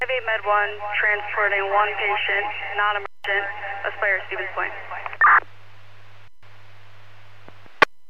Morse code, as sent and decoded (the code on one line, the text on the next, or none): none